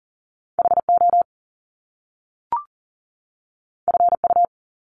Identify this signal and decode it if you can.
Morse code, heard as a keyed tone, about 38 wpm, 700 Hz. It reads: HO FU
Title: Top Two Letter Combinations in Words 40wpm